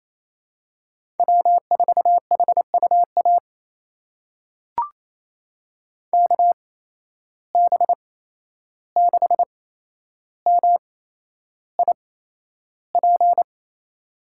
Morse code, heard as a keyed tone, about 28 wpm, spaced as for 11 wpm, 700 Hz. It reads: W4HUA KB6MIP
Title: USA Call Signs - 8x Character Spacing - 28wpm